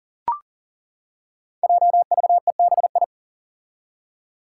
Morse code, heard as a keyed tone, about 40 wpm, 700 Hz. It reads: JVEBI
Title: Five Challenging Letter-Number Combinations 40wpm